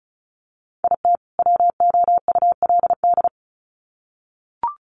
Morse code, heard as a keyed tone, about 34 wpm, 700 Hz. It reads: IT WOULD
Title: Sets of 2 Words 35wpm